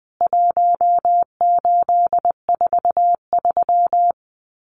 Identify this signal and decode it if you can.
Morse code, heard as a keyed tone, about 20 wpm, 700 Hz. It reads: 1843
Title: CWT Exchange 20wpm